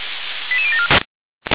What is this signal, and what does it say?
Five-tone selective call (ZVEI), heard as quick runs of five tones, tones of about 70 ms. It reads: 88074